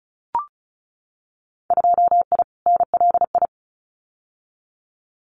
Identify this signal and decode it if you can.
Morse code, heard as a keyed tone, about 34 wpm, 700 Hz. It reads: 2I NLI
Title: Winter Field Day Exchanges 35wpm